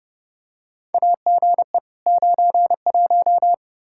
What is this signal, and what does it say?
Morse code, heard as a keyed tone, about 31 wpm, 700 Hz. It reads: AGE 91
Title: Practice QSO - Exchange 3 - More Info - 30wpm